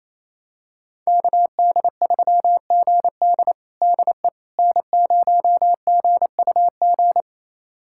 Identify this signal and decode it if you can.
Morse code, heard as a keyed tone, about 28 wpm, 700 Hz. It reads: KD3GD DE N0GUG
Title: USA Callsign DE Callsign - 5 Characters - 28wpm